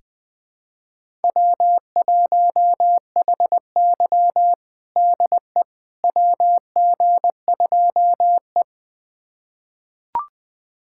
Morse code, heard as a keyed tone, about 21 wpm, 700 Hz. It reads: W1HY DE WG2E
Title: USA Callsign DE Callsign - 4 Characters - 20wpm